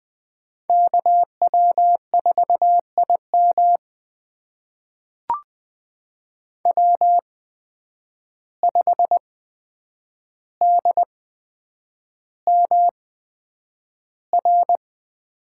Morse code, heard as a keyed tone, about 20 wpm, spaced as for 9 wpm, 700 Hz. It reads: KW4IM W5DMR
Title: USA Call Signs - 5 Characters - 8x Character Spacing - 20wpm